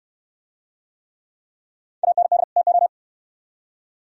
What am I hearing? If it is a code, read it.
Morse code, heard as a keyed tone, about 67 wpm, 700 Hz. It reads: AND TO